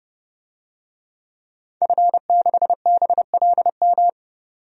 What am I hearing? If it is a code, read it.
Morse code, heard as a keyed tone, about 30 wpm, 700 Hz. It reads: F6BLM
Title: DXCC-100 Call Signs - 5 Characters - 30wpm